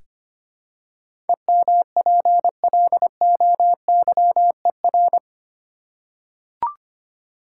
Morse code, heard as a keyed tone, about 25 wpm, 700 Hz. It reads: EMPLOYER
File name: Eight Letter Words - 25wpm